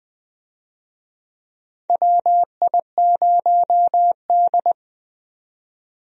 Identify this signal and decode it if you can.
Morse code, heard as a keyed tone, about 21 wpm, 700 Hz. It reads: WI0D
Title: USA Call Signs - 4 Characters - 20wpm